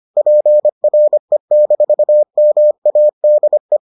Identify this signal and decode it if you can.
Morse code, heard as a keyed tone, about 25 wpm, 600 Hz. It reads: PRE-MADE